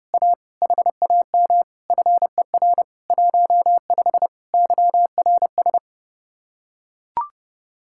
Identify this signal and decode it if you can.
Morse code, heard as a keyed tone, about 32 wpm, 700 Hz. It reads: A HAM FER 15 YRS